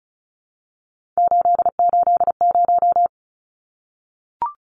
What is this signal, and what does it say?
Morse code, heard as a keyed tone, about 34 wpm, 700 Hz. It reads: 880
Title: Numbers - 3-Digit 35wpm